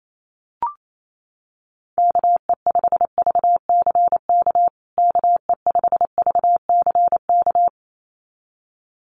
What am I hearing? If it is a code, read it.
Morse code, heard as a keyed tone, about 28 wpm, 700 Hz. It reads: KE5VCK KE5VCK
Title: USA Call Signs Repeated 2x 28wpm